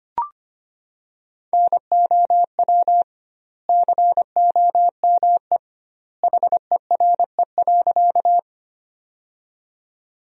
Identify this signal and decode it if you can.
Morse code, heard as a keyed tone, about 25 wpm, 700 Hz. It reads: NOW COME HERE.
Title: 2x Word-Spacing - Sentences from Top 200 Words - 25wpm